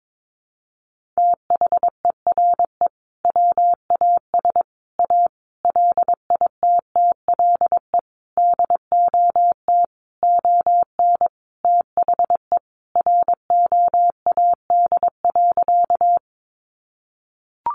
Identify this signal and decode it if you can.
Morse code, heard as a keyed tone, about 22 wpm, 700 Hz. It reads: THERE WAS A LITTLE DOT ON THE ROAD.
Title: Sentences from Top 1500 Words 22wpm